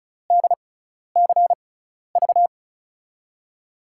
Morse code, heard as a keyed tone, about 34 wpm, spaced as for 12 wpm, 700 Hz. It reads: DCV
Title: USA Call Signs - 6x Character Spacing - 35wpm